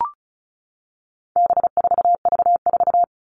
Morse code, heard as a keyed tone, about 34 wpm, 700 Hz. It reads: B4V4
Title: All Cut Numbers - 4-Digit 35wpm